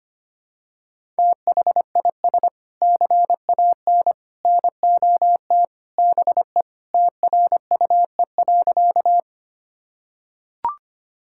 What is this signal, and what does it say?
Morse code, heard as a keyed tone, about 25 wpm, 700 Hz. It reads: THIS CAN NOT BE TRUE.
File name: practice / 25wpm / Sentences with 3-Word Phrases - 25wpm